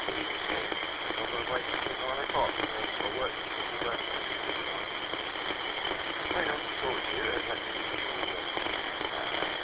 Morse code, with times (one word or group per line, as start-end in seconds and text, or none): none